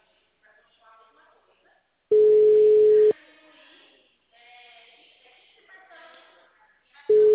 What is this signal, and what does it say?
Ringback tone (Europe, 425 Hz), with speech.